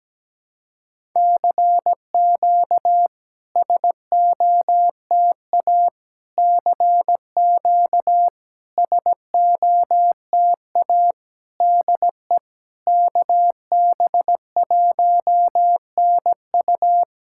Morse code, heard as a keyed tone, about 17 wpm, 700 Hz. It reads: CQ SOTA CQ SOTA DE KB1NU
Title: SOTA QSO Practice 17wpm